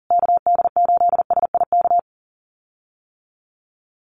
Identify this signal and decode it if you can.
Morse code, heard as a keyed tone, about 40 wpm, 700 Hz. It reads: KD8SIK
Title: USA Call Signs Repeated 3x 40wpm